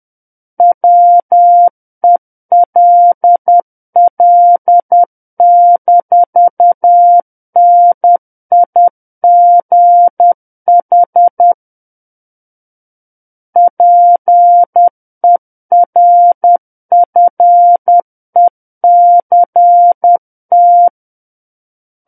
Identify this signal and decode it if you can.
Morse code, heard as a keyed tone, about 10 wpm, 700 Hz. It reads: WELL-NIGH PERFECT
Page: War of the Worlds - 01-Chapter 1 - 10 WPM